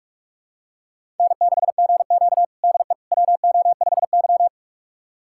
Morse code, heard as a keyed tone, about 46 wpm, 700 Hz. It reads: N6GQ DE WO5Y
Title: USA Callsign DE Callsign - 4 Characters - 45wpm